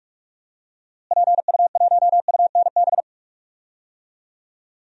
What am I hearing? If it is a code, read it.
Morse code, heard as a keyed tone, about 47 wpm, 700 Hz. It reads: PU1UNB
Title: DXCC-100 Call Signs - 6 Characters - Repeated 2x - 45wpm